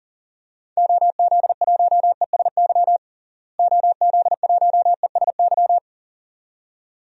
Morse code, heard as a keyed tone, about 40 wpm, 700 Hz. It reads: OZ1ESY OZ1ESY
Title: DXCC-100 Call Signs - 6 Characters - Repeated 2x - 40wpm